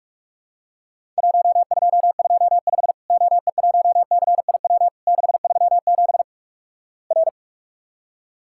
Morse code, heard as a keyed tone, about 46 wpm, 700 Hz. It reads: 1225 OE1CIW 637 R
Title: Worked All Europe DX Contest - QTC Exchange - 45wpm